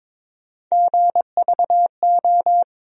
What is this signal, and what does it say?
Morse code, heard as a keyed tone, about 22 wpm, 700 Hz. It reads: GVO
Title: USA Call Sign Postfixes - 3 Letters 22wpm